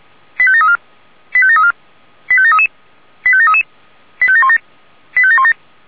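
Five-tone selective call (ZVEI), heard as quick runs of five tones, tones of about 70 ms. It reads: 85734 85734 85733 85733 85717 85717
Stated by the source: ZVEI five-tone call 85748 received on SDR.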